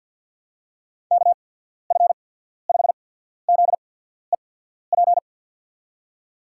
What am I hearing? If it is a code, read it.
Morse code, heard as a keyed tone, about 53 wpm, spaced as for 14 wpm, 700 Hz. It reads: KF5ZEP
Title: USA Call Signs - 6 Characters - 8x Character Spacing - 50wpm